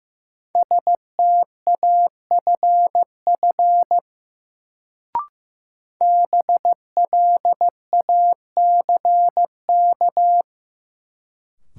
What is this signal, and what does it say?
Morse code, heard as a keyed tone, about 15 wpm, 700 Hz. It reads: STAFF BLACK
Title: Sets of 5 Words Encore - New Word Drill - 15wpm